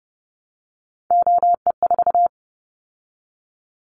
Morse code, heard as a keyed tone, about 30 wpm, 700 Hz. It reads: OE4
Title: Callsign Prefixes for Top 100 DXCC - 30wpm